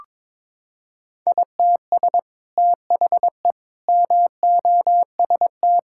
Morse code, heard as a keyed tone, about 22 wpm, 700 Hz. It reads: ITS THE MOST
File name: practice / 22wpm / Sets of 8 Words - 22wpm